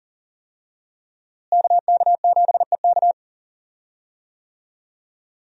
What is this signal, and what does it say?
Morse code, heard as a keyed tone, about 40 wpm, 700 Hz. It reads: KK7EK
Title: USA Call Signs - 5 Characters - Repeated 2x - 40wpm